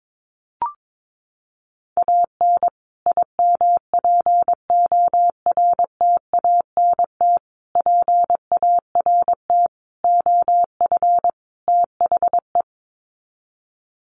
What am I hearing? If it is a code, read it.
Morse code, heard as a keyed tone, about 22 wpm, 700 Hz. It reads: AN IMPORTANT PART OF THE
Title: Sets of 5 Words Encore 22wpm